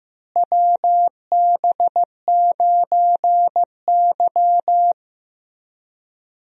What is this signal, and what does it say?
Morse code, heard as a keyed tone, about 15 wpm, 700 Hz. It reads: WB9Y
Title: USA Call Signs - 4 Characters - Repeated 3x - 15wpm